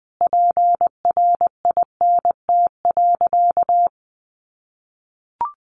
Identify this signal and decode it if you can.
Morse code, heard as a keyed tone, about 19 wpm, 700 Hz. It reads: PRINT.
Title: Sentences from Top 1400 Words 20wpm